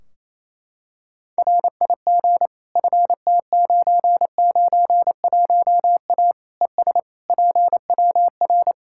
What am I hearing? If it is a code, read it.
Morse code, heard as a keyed tone, about 27 wpm, 700 Hz. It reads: RIG FT991A ES PWR
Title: Practice QSO - Exchange 2 - Info - 28wpm